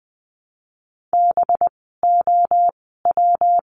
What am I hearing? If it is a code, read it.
Morse code, heard as a keyed tone, about 20 wpm, spaced as for 14 wpm, 700 Hz. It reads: BOW
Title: 185 Lesson 30f - Letter b - Words 20wpm